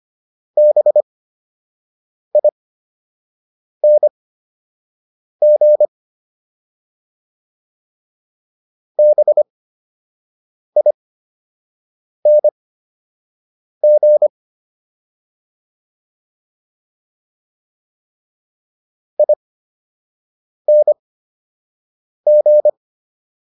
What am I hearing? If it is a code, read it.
Morse code, heard as a keyed tone, about 25 wpm, spaced as for 6 wpm, 600 Hz. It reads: BING BING ING